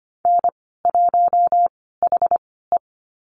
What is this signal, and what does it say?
Morse code, heard as a keyed tone, about 25 wpm, spaced as for 16 wpm, 700 Hz. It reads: N1HE
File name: Lesson 15g - Number 5 - Call Signs 25wpm